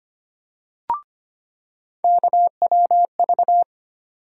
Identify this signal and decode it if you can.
Morse code, heard as a keyed tone, about 25 wpm, 700 Hz. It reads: KWV
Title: USA Call Sign Postfixes - 3 Letters 25wpm